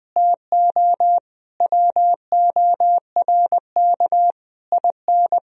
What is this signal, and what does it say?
Morse code, heard as a keyed tone, about 20 wpm, 700 Hz. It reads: TO WORK IN
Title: Sentences from Top 1000 Words - Review Part 3 - 20wpm